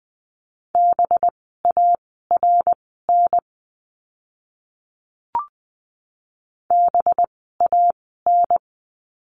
Morse code, heard as a keyed tone, about 20 wpm, spaced as for 14 wpm, 700 Hz. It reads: BARN BAN
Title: 185 Lesson 30f - Letter b - Words 20wpm